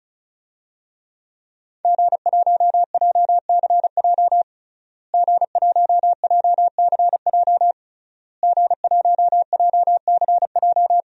Morse code, heard as a keyed tone, about 34 wpm, 700 Hz. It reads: G1JCJ G1JCJ G1JCJ
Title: DXCC-100 Call Signs - 5 Characters - Repeated 3x - 35wpm